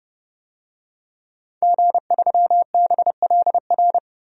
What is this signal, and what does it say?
Morse code, heard as a keyed tone, about 30 wpm, 700 Hz. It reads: G3BLR
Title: DXCC-100 Call Signs - 5 Characters - Repeated 3x - 30wpm